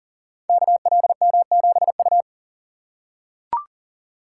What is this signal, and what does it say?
Morse code, heard as a keyed tone, about 40 wpm, 700 Hz. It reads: KLM7U
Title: Code Groups - 5 Letter:Numbers - 40wpm